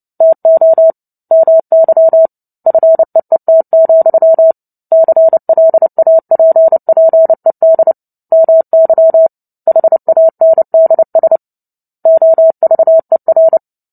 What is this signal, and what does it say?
Morse code, heard as a keyed tone, about 30 wpm, 650 Hz. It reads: TO MY FEET, CLAPPED MY HANDS OVER
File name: Book Two / 03-Chapter 3 - 29 WPM